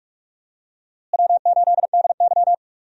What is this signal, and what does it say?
Morse code, heard as a keyed tone, about 48 wpm, 700 Hz. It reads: W8DY
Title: USA Call Signs - 4 Characters - Repeated 2x - 45wpm